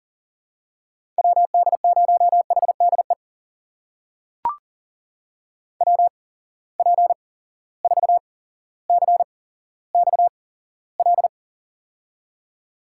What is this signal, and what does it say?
Morse code, heard as a keyed tone, about 40 wpm, spaced as for 17 wpm, 700 Hz. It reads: WD0HDE WP4CXL
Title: USA Call Signs - 6 Characters - 8x Character Spacing - 40wpm